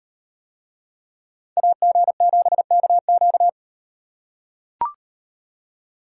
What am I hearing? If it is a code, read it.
Morse code, heard as a keyed tone, about 36 wpm, 700 Hz. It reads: AG7KQ